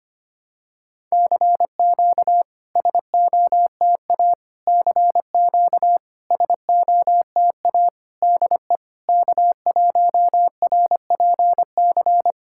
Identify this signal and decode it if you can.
Morse code, heard as a keyed tone, about 25 wpm, 700 Hz. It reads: CQ SOTA CQ SOTA DE K1RPC
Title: SOTA QSO Practice 25wpm